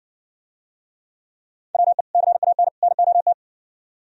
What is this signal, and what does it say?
Morse code, heard as a keyed tone, about 60 wpm, 700 Hz. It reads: WE CAN NOT